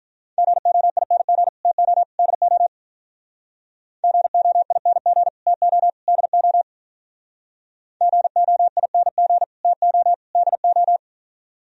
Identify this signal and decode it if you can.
Morse code, heard as a keyed tone, about 45 wpm, 700 Hz. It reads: GOING TO DO GOING TO DO GOING TO DO